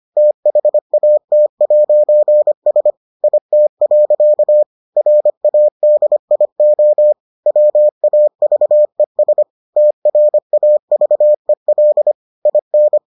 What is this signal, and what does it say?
Morse code, heard as a keyed tone, about 25 wpm, 600 Hz. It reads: THAT'S IT. RADIO WAVES TRAVEL IN